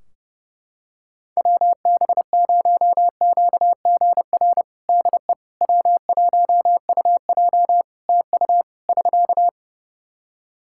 Morse code, heard as a keyed tone, about 30 wpm, 700 Hz. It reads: WB0QGR DE W1UJ TU <SK>
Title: Practice QSO - Exchange 4 - Ending - 30wpm